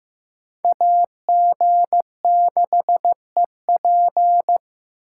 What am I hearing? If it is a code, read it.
Morse code, heard as a keyed tone, about 18 wpm, 700 Hz. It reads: AG6EP
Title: Speed Racing - USA 2x2 Call Signs 15wpm